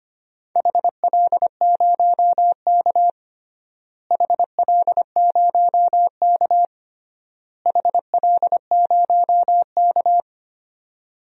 Morse code, heard as a keyed tone, about 25 wpm, 700 Hz. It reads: HL0K HL0K HL0K
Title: DXCC-100 Call Signs - 4 Characters - Repeated 3x - 25wpm